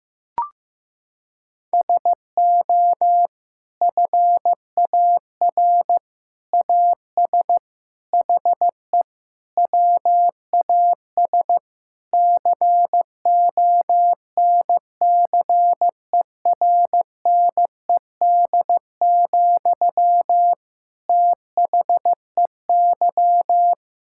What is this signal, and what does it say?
Morse code, heard as a keyed tone, about 15 wpm, 700 Hz. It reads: SO FAR AS HE WAS CONCERNED, THEY
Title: Sentences from Top 1000 Words - Review Part 6 - 15wpm